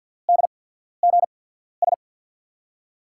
Morse code, heard as a keyed tone, about 51 wpm, spaced as for 14 wpm, 700 Hz. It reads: DGS
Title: USA Call Signs - 5 Characters - 8x Character Spacing - 50wpm